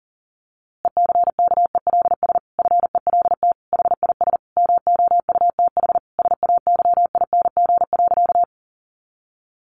Morse code, heard as a keyed tone, about 40 wpm, 700 Hz. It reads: ECKELS FELT HIS MOUTH SAYING.
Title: A Sound Of Thunder 40wpm